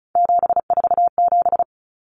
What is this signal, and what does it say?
Morse code, heard as a keyed tone, about 34 wpm, 700 Hz. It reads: 747
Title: Numbers - 3-Digit 35wpm